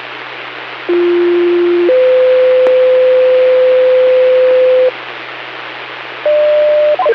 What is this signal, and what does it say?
Two-tone page, heard as a one-second tone, then a three-second tone, about 350 then 510 Hz, with noise. It is heard from 0.9 to 4.9 s.